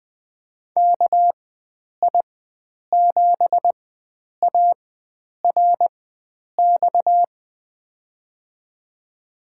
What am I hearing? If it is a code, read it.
Morse code, heard as a keyed tone, about 21 wpm, spaced as for 9 wpm, 700 Hz. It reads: KI7ARX
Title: USA Call Signs - 6 Characters - 4x Character Spacing - 20wpm